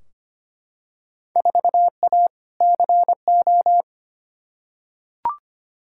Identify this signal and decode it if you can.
Morse code, heard as a keyed tone, about 25 wpm, 700 Hz. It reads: 4A CO
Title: Field Day Exchanges - 25wpm